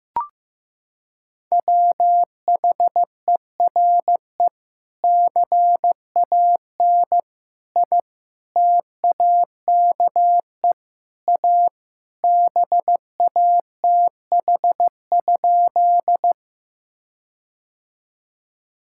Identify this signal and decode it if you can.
Morse code, heard as a keyed tone, about 15 wpm, 700 Hz. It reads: WHERE CAN I TAKE A BATH?
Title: Sentences from Top 1600 Words 15wpm